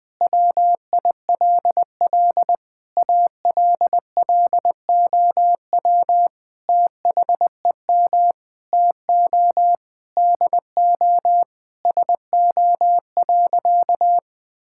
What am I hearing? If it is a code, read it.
Morse code, heard as a keyed tone, about 20 wpm, 700 Hz. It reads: WILL ALLOW THEM TO DO SO.
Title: Sentences from Top 1000 Words - Review Part 5 - 20wpm